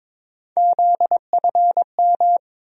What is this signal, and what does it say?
Morse code, heard as a keyed tone, about 21 wpm, 700 Hz. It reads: ZFM
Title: USA Call Sign Postfixes - 3 Letters 22wpm